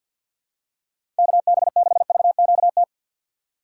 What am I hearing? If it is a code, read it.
Morse code, heard as a keyed tone, about 53 wpm, 700 Hz. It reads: KB6VQT
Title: USA Call Signs - 6 Characters - 3x Character Spacing - 50wpm